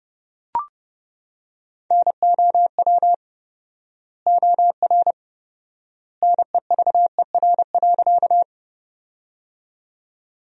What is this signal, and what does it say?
Morse code, heard as a keyed tone, about 30 wpm, 700 Hz. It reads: NOW OR NEVER.
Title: 4x Word-Spacing - Sentences from Top 300 Words - 30wpm